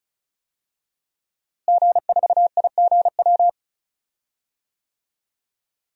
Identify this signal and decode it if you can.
Morse code, heard as a keyed tone, about 34 wpm, 700 Hz. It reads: G4IGW
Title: DXCC-100 Call Signs - 5 Characters - Repeated 2x - 35wpm